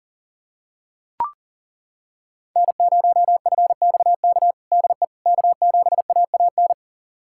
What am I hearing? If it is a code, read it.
Morse code, heard as a keyed tone, about 37 wpm, 700 Hz. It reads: N0FXK DE K7AAN
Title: USA Callsign DE Callsign - 5 Characters - 40wpm